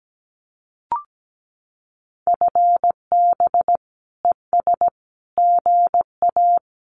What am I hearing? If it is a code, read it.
Morse code, heard as a keyed tone, about 17 wpm, 700 Hz. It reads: FB ES GA
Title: Common QSO Phrases Encore 17wpm